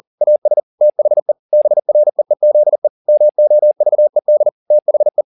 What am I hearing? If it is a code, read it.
Morse code, heard as a keyed tone, about 40 wpm, 600 Hz. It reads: AS THE BREEZE MOVED THE
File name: the old man and the sea 2 40wpm